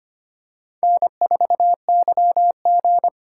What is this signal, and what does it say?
Morse code, heard as a keyed tone, about 25 wpm, 700 Hz. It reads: N4YG
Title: USA Call Signs - 4x Character Spacing - 25wpm